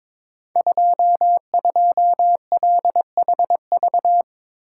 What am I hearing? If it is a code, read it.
Morse code, heard as a keyed tone, about 22 wpm, 690 Hz. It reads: 22LHV
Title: Code Groups - 5 Letter:Numbers - 22wpm